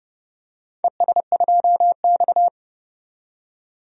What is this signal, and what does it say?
Morse code, heard as a keyed tone, about 32 wpm, 700 Hz. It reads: ES2X
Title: DXCC-100 Call Signs - 4 Characters - 30wpm